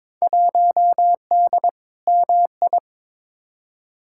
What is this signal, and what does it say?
Morse code, heard as a keyed tone, about 21 wpm, 700 Hz. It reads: 1D MI
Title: Field Day Exchanges - 22wpm